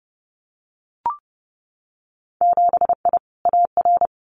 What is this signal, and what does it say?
Morse code, heard as a keyed tone, about 30 wpm, 700 Hz. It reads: 7I AR
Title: Winter Field Day Exchanges Encore 30wpm